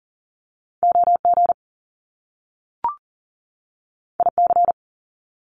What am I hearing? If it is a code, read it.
Morse code, heard as a keyed tone, about 39 wpm, 700 Hz. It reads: OG IC
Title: Top Two Letter Combinations in Words 40wpm